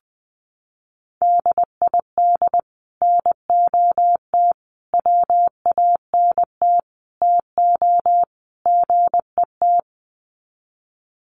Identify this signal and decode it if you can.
Morse code, heard as a keyed tone, about 20 wpm, 700 Hz. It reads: DID NOT WANT TO GET
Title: Sets of 5 Words Encore2 20wpm